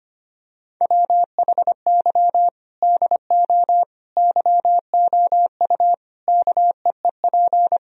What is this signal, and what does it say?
Morse code, heard as a keyed tone, about 25 wpm, 700 Hz. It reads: WHY DO YOU KEEP
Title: Sentences from Top 1000 Words - Review Part 3 - 25wpm